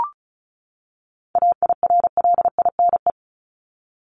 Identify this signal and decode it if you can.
Morse code, heard as a keyed tone, about 34 wpm, 700 Hz. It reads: AIRLINE
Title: Top 1501-1750 Words 35wpm